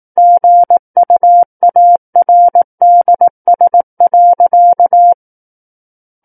Morse code, heard as a keyed tone, about 18 wpm, 700 Hz. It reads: GUARDS.
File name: Book One / 09-Chapter 9 - 18 WPM